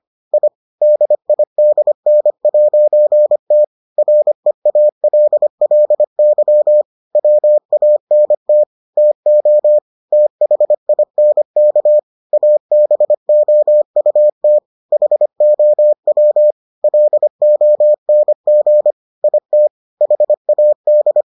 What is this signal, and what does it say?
Morse code, heard as a keyed tone, about 25 wpm, 600 Hz. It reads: I DIDN'T REALLY WANT TO THINK ABOUT HOW LONG IT HAD